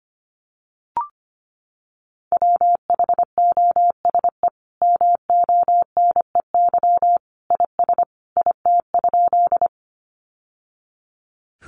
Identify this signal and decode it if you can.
Morse code, heard as a keyed tone, about 25 wpm, 700 Hz. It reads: WHOSE MONEY IS IT?
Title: Sentences from Top 1200 Words 25wpm